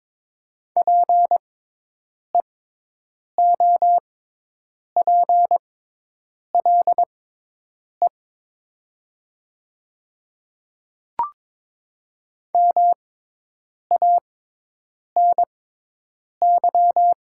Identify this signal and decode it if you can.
Morse code, heard as a keyed tone, about 22 wpm, spaced as for 8 wpm, 700 Hz. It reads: PEOPLE MANY
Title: 6x Character Spacing - Top 100 Words - 22wpm